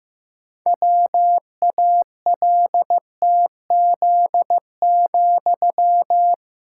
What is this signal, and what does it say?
Morse code, heard as a keyed tone, about 15 wpm, 700 Hz. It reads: WALTZ,
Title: Pangram Challenge v2 - 15wpm